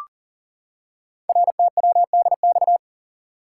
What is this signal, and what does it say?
Morse code, heard as a keyed tone, about 38 wpm, 690 Hz. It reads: RTWDX
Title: Code Groups - 5 Letters - 40wpm